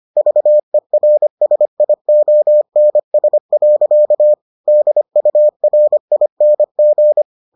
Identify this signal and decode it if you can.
Morse code, heard as a keyed tone, about 25 wpm, 600 Hz. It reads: VERSIONS. DURING